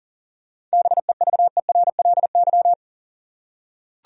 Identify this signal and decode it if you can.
Morse code, heard as a keyed tone, about 38 wpm, 700 Hz. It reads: BEVERLY
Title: USA Common Names - 40wpm